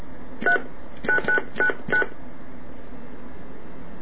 DTMF, heard as telephone touch tones, tones of about 100 ms. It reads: 39999